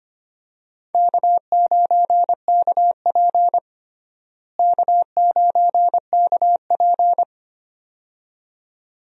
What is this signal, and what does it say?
Morse code, heard as a keyed tone, about 25 wpm, 700 Hz. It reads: K9KP K9KP
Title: USA Call Signs - 4 Characters - Repeated 2x - 25wpm